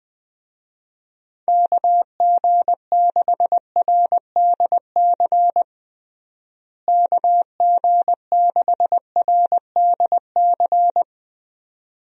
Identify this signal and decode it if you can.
Morse code, heard as a keyed tone, about 21 wpm, 700 Hz. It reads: KG6RDC KG6RDC
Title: USA Call Signs - 6 Characters - Repeated 2x - 20wpm